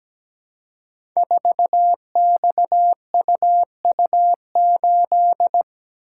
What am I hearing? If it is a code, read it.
Morse code, heard as a keyed tone, about 17 wpm, 700 Hz. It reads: 4XUU8
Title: Code Groups - 5 Letter:Numbers - 17wpm